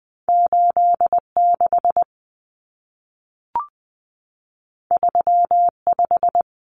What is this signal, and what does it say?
Morse code, heard as a keyed tone, about 20 wpm, 700 Hz. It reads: 86 35
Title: Common Cut Numbers - 2-Digit 20wpm